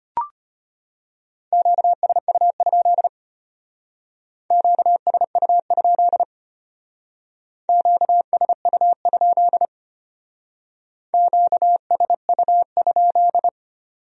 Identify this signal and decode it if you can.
Morse code, heard as a keyed tone, about 32 wpm, 700 Hz. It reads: QSU? QSU? QSU? QSU?